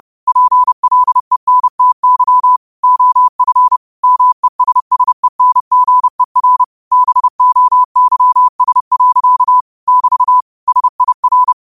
Morse code, heard as a keyed tone, about 30 wpm, 1000 Hz. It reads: PLENTY OF MESSENGER BOYS. <BT> SIR
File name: qotdfiles / 30wpm